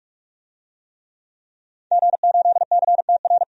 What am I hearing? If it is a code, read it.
Morse code, heard as a keyed tone, about 44 wpm, 700 Hz. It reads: G8CTR
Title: DXCC-100 Call Signs - 5 Characters - 45wpm